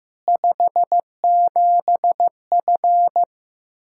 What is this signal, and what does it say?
Morse code, heard as a keyed tone, about 16 wpm, 700 Hz. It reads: 57F
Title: Practice QSO - Exchange 2 - Info - 15wpm